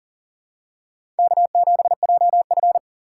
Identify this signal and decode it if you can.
Morse code, heard as a keyed tone, about 40 wpm, 700 Hz. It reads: K7JF
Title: USA Call Signs - 4 Characters - 6x Character Spacing - 40wpm